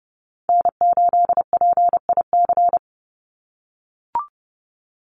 Morse code, heard as a keyed tone, about 30 wpm, 700 Hz. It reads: N8PIC
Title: USA Call Signs 30wpm